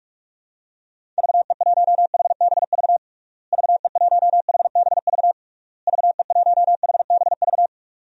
Morse code, heard as a keyed tone, about 46 wpm, 700 Hz. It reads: VE1HBV VE1HBV VE1HBV
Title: DXCC-100 Call Signs - 6 Characters - Repeated 3x - 45wpm